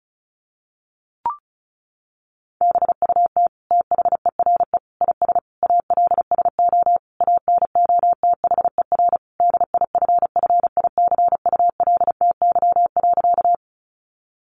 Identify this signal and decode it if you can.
Morse code, heard as a keyed tone, about 34 wpm, 700 Hz. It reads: BUT THERE IS ALSO ANOTHER DIFFICULTY.
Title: Sentences from Top 1500 Words 35wpm